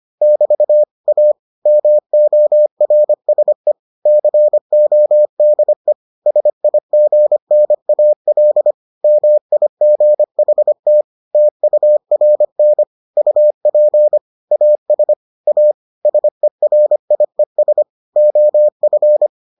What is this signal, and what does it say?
Morse code, heard as a keyed tone, about 25 wpm, 600 Hz. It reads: <BT> A MORSE CODE SIGNAL MIGHT TURN UP AS A SERIES OF